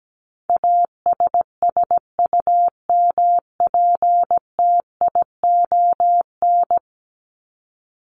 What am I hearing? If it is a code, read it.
Morse code, heard as a keyed tone, about 17 wpm, 700 Hz. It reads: ASSUMPTION
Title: Top 1751-2000 Words 17wpm